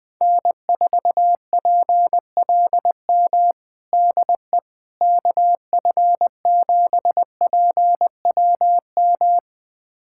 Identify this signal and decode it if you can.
Morse code, heard as a keyed tone, about 21 wpm, 700 Hz. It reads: N4PLM DE KF7PWM
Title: USA Callsign DE Callsign 20wpm